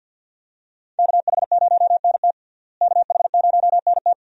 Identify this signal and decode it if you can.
Morse code, heard as a keyed tone, about 46 wpm, 700 Hz. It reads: KH0NT KH0NT
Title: USA Call Signs - 5 Characters - Repeated 2x - 50wpm